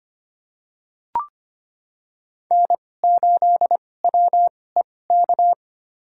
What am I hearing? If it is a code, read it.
Morse code, heard as a keyed tone, about 25 wpm, spaced as for 18 wpm, 700 Hz. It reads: N8WEK
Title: USA Call Signs - 2x Character Spacing - 25wpm